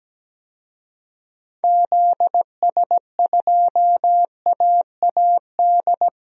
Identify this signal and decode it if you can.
Morse code, heard as a keyed tone, about 17 wpm, 700 Hz. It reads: ZS2AAD
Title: DXCC-100 Call Signs - 6 Characters - 17wpm